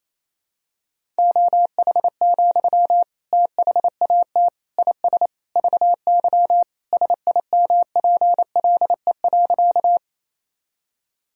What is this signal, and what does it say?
Morse code, heard as a keyed tone, about 28 wpm, 700 Hz. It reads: OH, THAT IS VY SIMPLE.
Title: Sentences from Top 500 Words and Abbreviations 28wpm